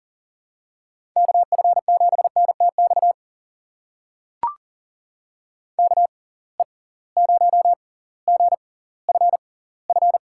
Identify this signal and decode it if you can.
Morse code, heard as a keyed tone, about 40 wpm, spaced as for 20 wpm, 700 Hz. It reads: KF7NTX KE0GFF
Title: USA Call Signs - 6 Characters - 6x Character Spacing - 40wpm